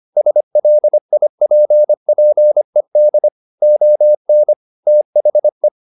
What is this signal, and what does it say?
Morse code, heard as a keyed tone, about 25 wpm, 600 Hz. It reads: SLIPPED ON THE
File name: the old man and the sea part 1 25wpm